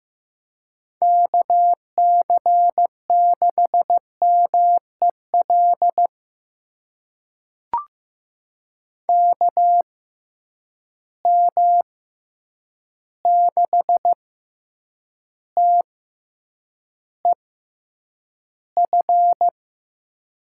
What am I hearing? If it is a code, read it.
Morse code, heard as a keyed tone, about 15 wpm, spaced as for 8 wpm, 700 Hz. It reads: KC6MEL KM6TEF
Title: USA Call Signs - 6 Characters - 6x Character Spacing - 15wpm